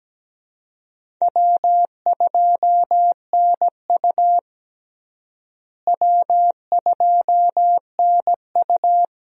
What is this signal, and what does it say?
Morse code, heard as a keyed tone, about 17 wpm, 700 Hz. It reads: W2NU W2NU
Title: USA Call Signs - 4 Characters - Repeated 3x - 17wpm